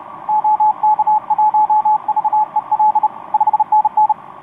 Morse code, heard as a keyed tone, about 32 wpm, 860 Hz, with radio noise. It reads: OK1VEL 5NN
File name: t46rrc i2uua etc 12m cw 18feb2013 1528